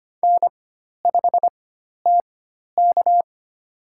Morse code, heard as a keyed tone, about 25 wpm, spaced as for 12 wpm, 700 Hz. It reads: N5TK
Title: USA Call Signs - 4 Characters - 4x Character Spacing - 25wpm